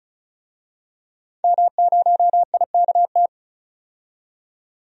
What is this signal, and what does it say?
Morse code, heard as a keyed tone, about 34 wpm, 700 Hz. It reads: M0IKT
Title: DXCC-100 Call Signs - 5 Characters - 35wpm